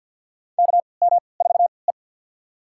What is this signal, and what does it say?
Morse code, heard as a keyed tone, about 53 wpm, spaced as for 29 wpm, 700 Hz. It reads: KM4E
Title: USA Call Signs - 4 Characters - 3x Character Spacing - 50wpm